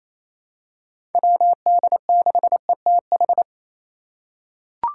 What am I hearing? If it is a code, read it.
Morse code, heard as a keyed tone, about 27 wpm, 700 Hz. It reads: WD6ETH
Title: USA Call Signs - 6 Characters - 2x Character Spacing - 28wpm